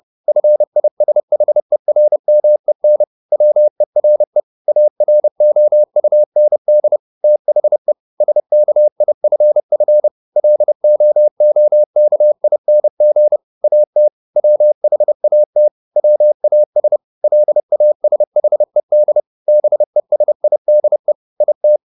Morse code, heard as a keyed tone, about 30 wpm, 600 Hz. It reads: FISHERMEN WERE AROUND THE SKIFF LOOKING AT WHAT WAS LASHED BESIDE IT